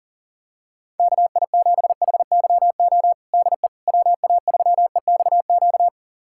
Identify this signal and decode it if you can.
Morse code, heard as a keyed tone, about 40 wpm, 700 Hz. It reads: KI7HYO DE WA3EXQ
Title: USA Callsign DE Callsign - 6 Characters - 40wpm